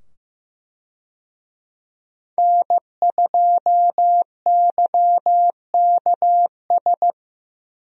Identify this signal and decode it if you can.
Morse code, heard as a keyed tone, about 15 wpm, 700 Hz. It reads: N2YKS
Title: USA Call Signs - 5 Characters - 15wpm